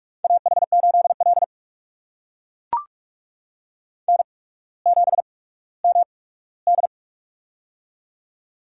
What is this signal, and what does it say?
Morse code, heard as a keyed tone, about 44 wpm, spaced as for 19 wpm, 700 Hz. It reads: AH8L N7MD
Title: USA Call Signs - 4 Characters - 8x Character Spacing - 45wpm